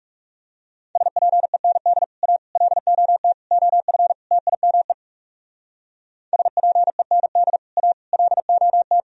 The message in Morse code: SPEND A LOT OF TIME SPEND A LOT

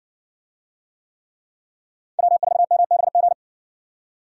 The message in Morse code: W4MBG

